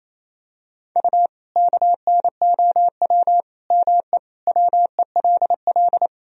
Morse code, heard as a keyed tone, about 27 wpm, 700 Hz. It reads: U KNOW ME WELL
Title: Sentences from Top 400 Words and Abbreviations 28wpm